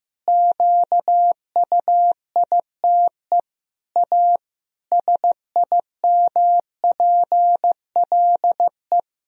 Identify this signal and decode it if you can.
Morse code, heard as a keyed tone, about 15 wpm, 700 Hz. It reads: QUITE A SIMPLE
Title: Sentences from Top 1000 Words - Review Part 3 - 15wpm